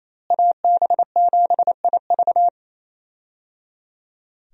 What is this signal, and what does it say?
Morse code, heard as a keyed tone, about 28 wpm, 700 Hz. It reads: AB7IV
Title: USA Call Signs - 5 Characters - Repeated 3x - 28wpm